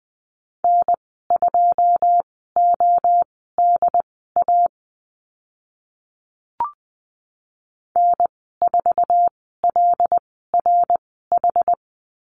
Morse code, heard as a keyed tone, about 20 wpm, spaced as for 14 wpm, 700 Hz. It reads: N2ODA N4LRH
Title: 088 Lesson 15g - Number 5 - Call Signs 20wpm